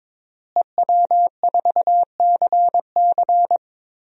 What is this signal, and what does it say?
Morse code, heard as a keyed tone, about 22 wpm, 700 Hz. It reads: EW4CC
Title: DXCC-100 Call Signs - 5 Characters - Repeated 3x - 22wpm